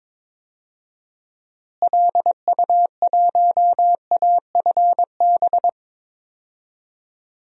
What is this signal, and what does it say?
Morse code, heard as a keyed tone, about 22 wpm, 700 Hz. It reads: LU1AFB
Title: DXCC-100 Call Signs - 6 Characters - Repeated 3x - 22wpm